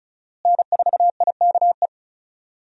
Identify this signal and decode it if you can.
Morse code, heard as a keyed tone, about 34 wpm, 700 Hz. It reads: N4IKE
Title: USA Call Signs - 5 Characters - Repeated 3x - 35wpm